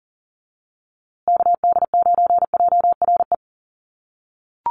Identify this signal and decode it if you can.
Morse code, heard as a keyed tone, about 40 wpm, 700 Hz. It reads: KD9JRE
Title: USA Call Signs Repeated 2x 40wpm